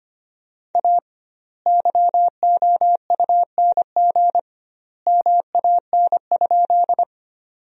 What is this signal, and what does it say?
Morse code, heard as a keyed tone, about 25 wpm, 700 Hz. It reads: A YOUNG MAN?
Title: 2x Word-Spacing - Sentences from Top 400 Words - 25wpm